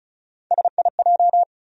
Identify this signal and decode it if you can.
Morse code, heard as a keyed tone, about 34 wpm, 700 Hz. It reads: SIJ
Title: USA Call Sign Postfixes - 3 Letters 35wpm